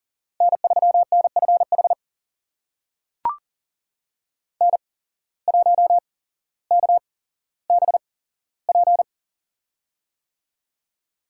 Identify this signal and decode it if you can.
Morse code, heard as a keyed tone, about 40 wpm, spaced as for 17 wpm, 700 Hz. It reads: N3NFH N1KBP
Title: USA Call Signs - 5 Characters - 8x Character Spacing - 40wpm